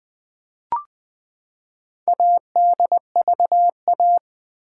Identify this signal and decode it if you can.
Morse code, heard as a keyed tone, about 20 wpm, 700 Hz. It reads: ADVA
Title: Build A Word - 4 Letters At A Time - 8-Letter Words - 20wpm